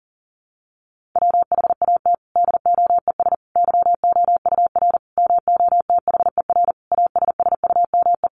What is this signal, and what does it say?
Morse code, heard as a keyed tone, about 40 wpm, 700 Hz. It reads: WHAT DOES YOUR MOTHER ASSUME
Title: Sentences from Top 1500 Words 40wpm